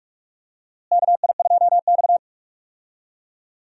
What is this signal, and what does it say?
Morse code, heard as a keyed tone, about 42 wpm, 700 Hz. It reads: KI2X
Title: USA Call Signs - 4 Characters - Repeated 3x - 45wpm